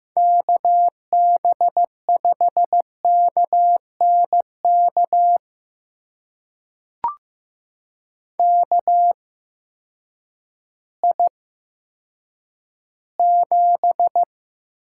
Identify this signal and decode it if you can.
Morse code, heard as a keyed tone, about 15 wpm, spaced as for 9 wpm, 700 Hz. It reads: KB5KNK KI7
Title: USA Call Signs - 6 Characters - 8x Character Spacing - 15wpm